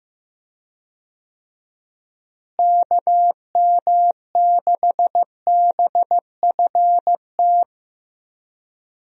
Morse code, heard as a keyed tone, about 16 wpm, 700 Hz. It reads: KM6BFT
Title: USA Call Signs - 6 Characters - Repeated 3x - 15wpm